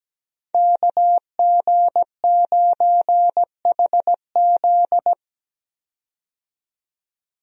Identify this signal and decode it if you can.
Morse code, heard as a keyed tone, about 17 wpm, 700 Hz. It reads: KG9HZ
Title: USA Call Signs - 5 Characters - 17wpm